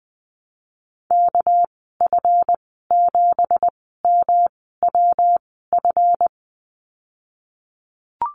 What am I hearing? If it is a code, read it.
Morse code, heard as a keyed tone, about 20 wpm, spaced as for 14 wpm, 700 Hz. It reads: KF7MWF
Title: 200 Lesson 32g - Letter k - Call Signs 20wpm